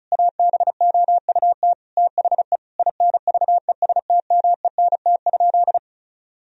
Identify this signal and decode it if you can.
Morse code, heard as a keyed tone, about 36 wpm, 700 Hz. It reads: ABOUT THE INVESTMENT?
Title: Sentences from Top 1000 Words - Review Part 4 - 35wpm